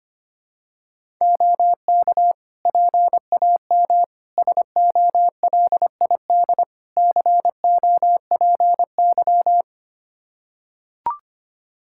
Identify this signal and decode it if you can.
Morse code, heard as a keyed tone, about 25 wpm, 700 Hz. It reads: OK PAM SOLID COPY
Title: K1USN SST QSO Practice 25wpm